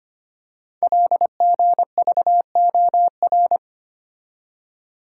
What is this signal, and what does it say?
Morse code, heard as a keyed tone, about 25 wpm, 700 Hz. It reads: LGVOR
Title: Code Groups - 5 Letters - 25wpm